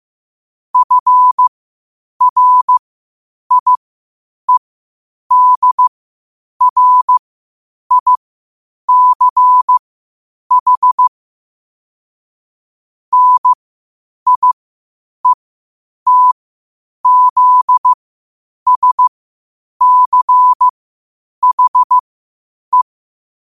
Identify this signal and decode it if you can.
Morse code, heard as a keyed tone, about 15 wpm, spaced as for 8 wpm, 1000 Hz. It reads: FRIEDRICH NIETZSCHE
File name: qotdfiles / 8wpm